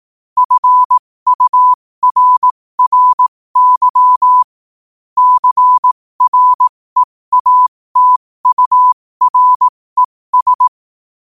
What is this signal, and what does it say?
Morse code, heard as a keyed tone, about 18 wpm, spaced as for 15 wpm, 1000 Hz. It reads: FURRY CREATURES
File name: qotdfiles / 15wpm